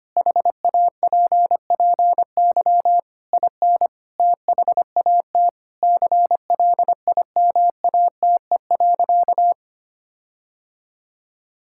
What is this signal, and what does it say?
Morse code, heard as a keyed tone, about 25 wpm, 700 Hz. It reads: HAPPY IN THAT CLIMATE.
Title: Sentences from Top 1600 Words 25wpm